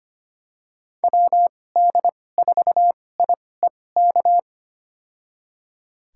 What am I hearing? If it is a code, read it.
Morse code, heard as a keyed tone, about 25 wpm, spaced as for 18 wpm, 690 Hz. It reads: WD4IEK
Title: USA Call Signs - 6 Characters - 2x Character Spacing - 25wpm